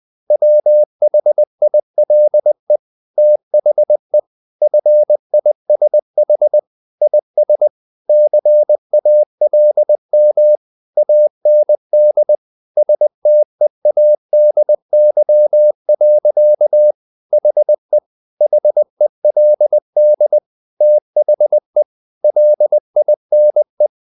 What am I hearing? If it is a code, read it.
Morse code, heard as a keyed tone, about 20 wpm, 600 Hz. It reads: WHILE THE FISH IS CALM AND STEADY. HE HELD THE LINE
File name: the old man and the sea part 2 20wpm